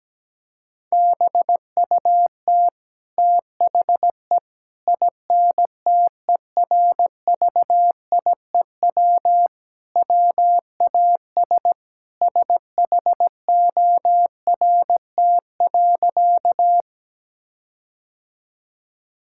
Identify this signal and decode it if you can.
Morse code, heard as a keyed tone, about 17 wpm, 700 Hz. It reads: BUT THE INTERVIEW WAS SHORT.
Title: Sentences from Top 1600 Words 17wpm